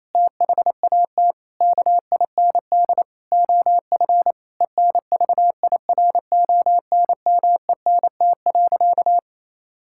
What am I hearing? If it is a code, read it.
Morse code, heard as a keyed tone, about 27 wpm, 700 Hz. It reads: THAT KIND OF ENVIRONMENT.
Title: Sentences from Top 1000 Words - Review Part 5 - 28wpm